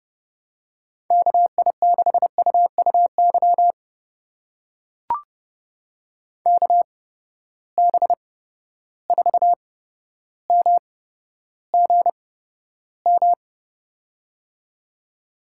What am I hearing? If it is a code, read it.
Morse code, heard as a keyed tone, about 32 wpm, spaced as for 13 wpm, 700 Hz. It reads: KI6UUY KB4MGM
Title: USA Call Signs - 6 Characters - 8x Character Spacing - 30wpm